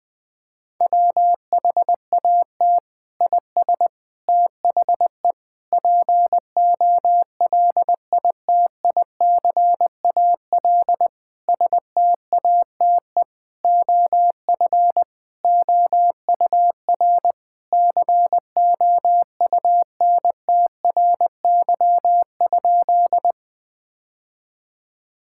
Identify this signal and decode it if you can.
Morse code, heard as a keyed tone, about 20 wpm, 700 Hz. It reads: WHAT IS THE POLITICAL STATE OF OUR COUNTRY?
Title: Sentences from Top 1000 Words - Review Part 5 - 20wpm